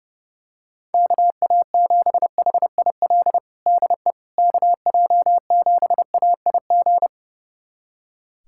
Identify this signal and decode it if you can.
Morse code, heard as a keyed tone, about 30 wpm, 700 Hz. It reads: KA7HIL DE KJ7AIG
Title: USA Callsign DE Callsign - 6 Characters - 30wpm